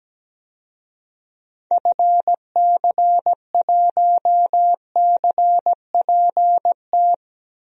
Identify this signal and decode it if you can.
Morse code, heard as a keyed tone, about 17 wpm, 700 Hz. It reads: FC1CPT
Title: DXCC-100 Call Signs - 6 Characters - 17wpm